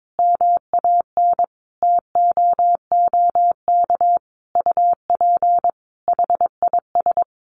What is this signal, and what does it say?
Morse code, heard as a keyed tone, about 22 wpm, 700 Hz. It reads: MAN TOOK UP HIS
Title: Sentences from Top 1500 Words 22wpm